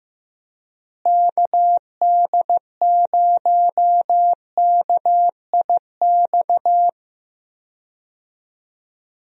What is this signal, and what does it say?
Morse code, heard as a keyed tone, about 15 wpm, 700 Hz. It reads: KD0KIX
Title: USA Call Signs - 6 Characters - Repeated 3x - 15wpm